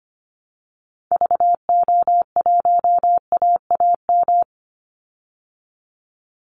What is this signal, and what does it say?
Morse code, heard as a keyed tone, about 25 wpm, 700 Hz. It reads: VO1AAM
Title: Call Signs Repeated 2x - 25wpm